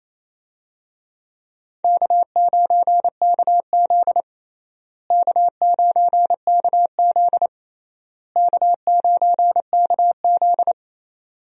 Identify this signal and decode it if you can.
Morse code, heard as a keyed tone, about 28 wpm, 700 Hz. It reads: K9KZ K9KZ K9KZ
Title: USA Call Signs - 4 Characters - Repeated 3x - 28wpm